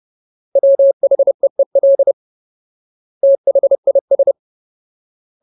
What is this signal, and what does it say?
Morse code, heard as a keyed tone, about 30 wpm, 550 Hz. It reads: WHEEL THIS
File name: Ultimate Word List / 30WPM Ultimate